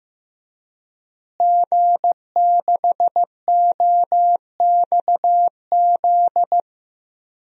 Morse code, heard as a keyed tone, about 15 wpm, 700 Hz. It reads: G6OXZ
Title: DXCC-100 Call Signs - 5 Characters - Repeated 3x - 15wpm